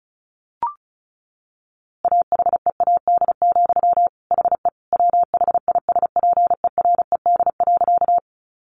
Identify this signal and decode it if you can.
Morse code, heard as a keyed tone, about 34 wpm, 700 Hz. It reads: AHEAD, HE WHISPERED.
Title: A Sound Of Thunder 35wpm